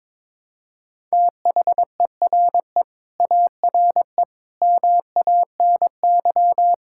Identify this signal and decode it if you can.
Morse code, heard as a keyed tone, about 22 wpm, 700 Hz. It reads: THERE ARE MANY
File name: practice / 22wpm / Sentences with 2-Word Phrases - 22wpm